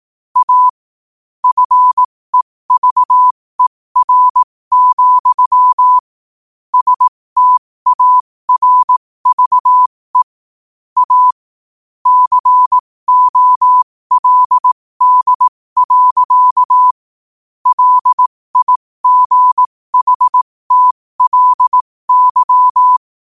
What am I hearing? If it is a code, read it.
Morse code, heard as a keyed tone, about 18 wpm, spaced as for 15 wpm, 1000 Hz. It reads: A FEVER, STARVE A COLD. LIGHTLY
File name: qotdfiles / 15wpm